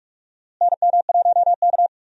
Morse code, heard as a keyed tone, about 43 wpm, 700 Hz. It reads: NM1K
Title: USA Call Signs - 4 Characters - Repeated 3x - 45wpm